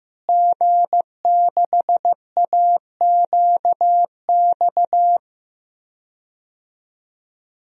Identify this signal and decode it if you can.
Morse code, heard as a keyed tone, about 15 wpm, 700 Hz. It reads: G6AQX
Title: DXCC-100 Call Signs - 5 Characters - Repeated 3x - 15wpm